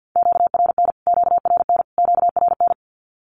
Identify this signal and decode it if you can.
Morse code, heard as a keyed tone, about 50 wpm, 700 Hz. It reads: QRN QRN QRN
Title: Q-Codes Repeated 3x 50wpm